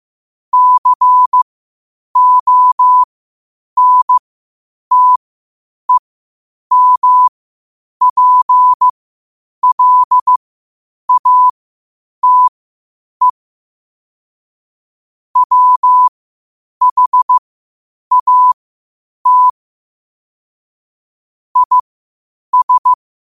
Morse code, heard as a keyed tone, about 15 wpm, spaced as for 8 wpm, 1000 Hz. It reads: CONTEMPLATE WHAT IS
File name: qotdfiles / 8wpm